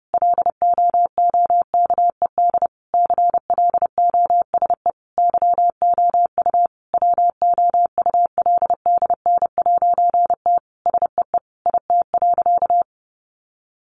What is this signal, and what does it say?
Morse code, heard as a keyed tone, about 30 wpm, 700 Hz. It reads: LOOKED CLOSE YOU WOULDN'T SEE IT.
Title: A Sound Of Thunder 30wpm